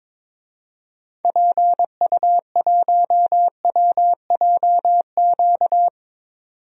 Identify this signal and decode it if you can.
Morse code, heard as a keyed tone, about 22 wpm, 700 Hz. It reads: PU1WJQ
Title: DXCC-100 Call Signs - 6 Characters - 22wpm